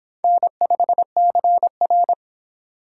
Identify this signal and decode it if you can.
Morse code, heard as a keyed tone, about 24 wpm, 700 Hz. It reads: N5CR